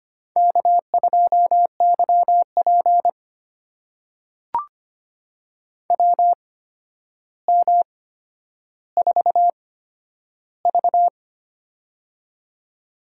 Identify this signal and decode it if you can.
Morse code, heard as a keyed tone, about 25 wpm, spaced as for 11 wpm, 700 Hz. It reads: K2YP WM4V
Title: USA Call Signs - 4 Characters - 8x Character Spacing - 25wpm